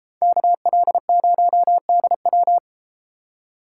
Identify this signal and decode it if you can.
Morse code, heard as a keyed tone, about 32 wpm, 690 Hz. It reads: KL0DW